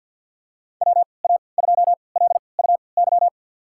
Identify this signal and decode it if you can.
Morse code, heard as a keyed tone, about 51 wpm, spaced as for 28 wpm, 700 Hz. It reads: WA2LUY
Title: USA Call Signs - 6 Characters - 3x Character Spacing - 50wpm